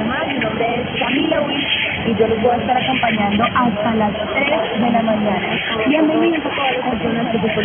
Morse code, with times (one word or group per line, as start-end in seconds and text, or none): none